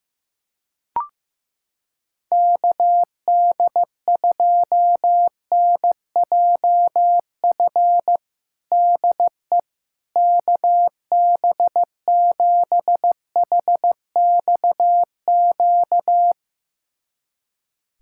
Morse code, heard as a keyed tone, about 15 wpm, 700 Hz. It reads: KD2NJF DE KB7HXQ
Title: USA Callsign DE Callsign - 6 Characters - 15wpm